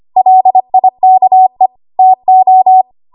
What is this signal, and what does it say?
Morse code, heard as a keyed tone, about 25 wpm, 750 Hz. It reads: LIKE TO
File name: Archive / 25 WPM / 25WPM